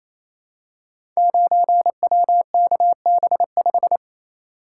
Morse code, heard as a keyed tone, about 27 wpm, 700 Hz. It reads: 9WKB5
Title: Code Groups - 5 Letter:Numbers - 28wpm